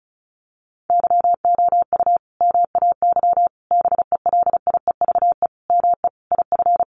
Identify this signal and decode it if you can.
Morse code, heard as a keyed tone, about 34 wpm, 700 Hz. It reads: YOU MAY BELIEVE ME IF
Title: Sentences from Top 1000 Words 35wpm